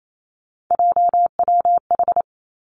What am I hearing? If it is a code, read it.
Morse code, heard as a keyed tone, about 28 wpm, 700 Hz. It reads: JWH
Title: Three Challenging Letter-Number Combinations 28wpm